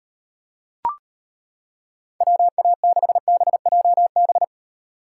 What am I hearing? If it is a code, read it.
Morse code, heard as a keyed tone, about 30 wpm, 700 Hz. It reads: WA6BJB
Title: Speed Racing - USA Call Signs 25wpm